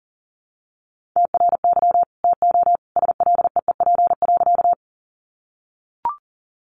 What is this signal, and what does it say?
Morse code, heard as a keyed tone, about 40 wpm, 700 Hz. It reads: TRY TO SLEEP.
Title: Sentences from Top 500 Words 40wpm